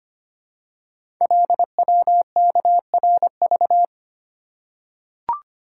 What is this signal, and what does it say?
Morse code, heard as a keyed tone, about 25 wpm, 700 Hz. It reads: LWKRV
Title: Code Groups - 5 Letters - 25wpm